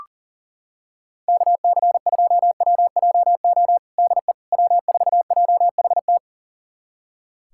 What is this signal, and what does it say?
Morse code, heard as a keyed tone, about 40 wpm, 700 Hz. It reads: KC2WJO DE W4JHT
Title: USA Callsign DE Callsign 40wpm